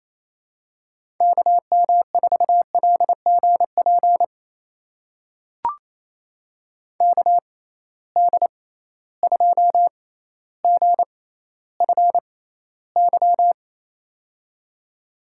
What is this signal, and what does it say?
Morse code, heard as a keyed tone, about 28 wpm, spaced as for 14 wpm, 700 Hz. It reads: KM4LGP KD2GFY
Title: USA Call Signs - 6 Characters - 6x Character Spacing - 28wpm